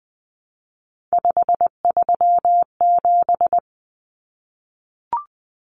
Morse code, heard as a keyed tone, about 20 wpm, 700 Hz. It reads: 537
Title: Common Cut Numbers - 3-Digit 20wpm